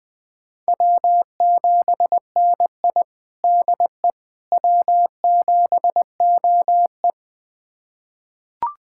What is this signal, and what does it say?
Morse code, heard as a keyed tone, about 20 wpm, 700 Hz. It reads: W7NI DE W7OE
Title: USA Callsign DE Callsign - 4 Characters - 20wpm